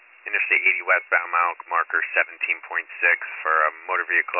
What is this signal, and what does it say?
Emergency Alert System (SAME) tones, heard as harsh buzzing bursts, with speech. It is heard from 0.8 to 3.3 s.